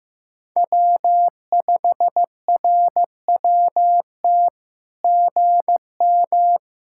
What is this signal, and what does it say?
Morse code, heard as a keyed tone, about 15 wpm, 700 Hz. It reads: W5RWT GM
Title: SOTA QSO Practice 15wpm